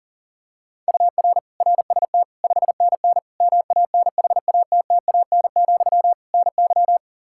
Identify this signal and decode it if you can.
Morse code, heard as a keyed tone, about 40 wpm, 700 Hz. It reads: UR RST 5NN MANHATTAN, NY
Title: Practice QSO - Exchange 1 - Beginning - 40wpm